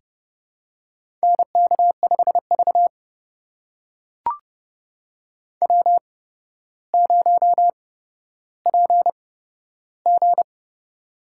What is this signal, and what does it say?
Morse code, heard as a keyed tone, about 29 wpm, spaced as for 13 wpm, 700 Hz. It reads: NK5V W0PG
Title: USA Call Signs - 4 Characters - 8x Character Spacing - 30wpm